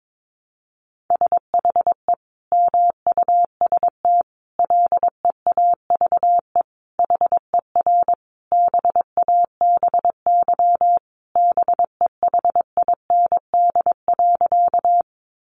Morse code, heard as a keyed tone, about 22 wpm, 700 Hz. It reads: SHE MUST LEAVE HER BABY BEHIND.
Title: Sentences from Top 1300 Words 22wpm